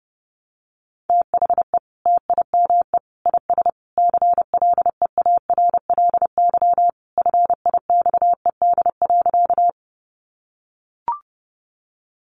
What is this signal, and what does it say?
Morse code, heard as a keyed tone, about 30 wpm, 700 Hz. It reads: THE TIME IS CLEARLY FIXED.
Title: Sentences from Top 1300 Words 30wpm